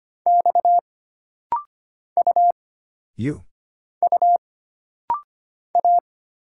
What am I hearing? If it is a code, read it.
Morse code, heard as a keyed tone, about 25 wpm, 700 Hz. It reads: X U U A
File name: Single Character (Letter:Number:Punctuation:Prosign) - Rapid-Fire - 25wpm